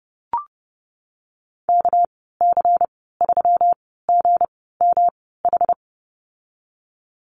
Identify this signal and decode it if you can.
Morse code, heard as a keyed tone, about 30 wpm, spaced as for 17 wpm, 700 Hz. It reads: KC3GMH
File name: Lesson 37g - Letter x - Call Signs 30wpm